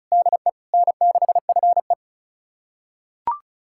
Morse code, heard as a keyed tone, about 34 wpm, 700 Hz. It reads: DE N6FE
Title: USA Callsign DE Callsign - 4 Characters - 35wpm